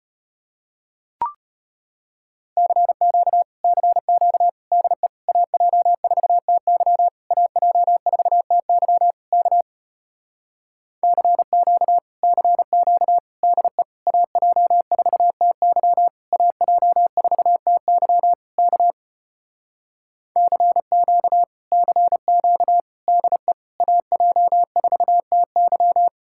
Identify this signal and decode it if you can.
Morse code, heard as a keyed tone, about 33 wpm, 700 Hz. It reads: CQ CQ DE AJ4TY AJ4TY K CQ CQ DE AJ4TY AJ4TY K CQ CQ DE AJ4TY